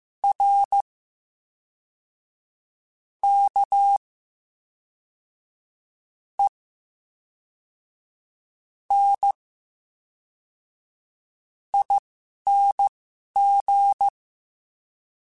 Morse code, heard as a keyed tone, about 15 wpm, spaced as for 11 wpm, 800 Hz. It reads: R K E N ING
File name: Special INGs / Special INGs-mp3s / VerySlowToFaster Ing @10